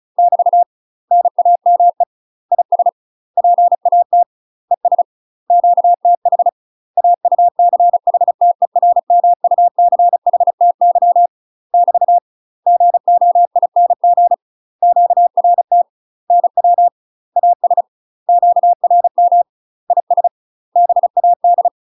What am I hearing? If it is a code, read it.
Morse code, heard as a keyed tone, about 34 wpm, 700 Hz. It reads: <BT> NAME IS PAT ES QTH AUCHTERMUCHTY <BT> GOING QRT NW AS QRM IS BAD
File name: UK MASTER QSOs TRAINING / 35WPM / UK MASTER QSOs 0005